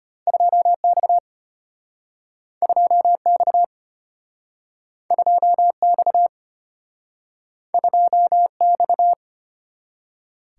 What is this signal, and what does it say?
Morse code, heard as a keyed tone, about 33 wpm, 700 Hz. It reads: 2X 2X 2X 2X